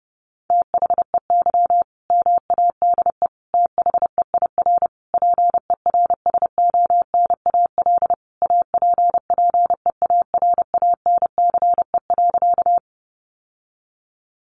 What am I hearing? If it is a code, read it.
Morse code, heard as a keyed tone, about 30 wpm, 700 Hz. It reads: THEY MADE THEIR PERSONAL APPEARANCE.
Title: Sentences from Top 1100 Words 30wpm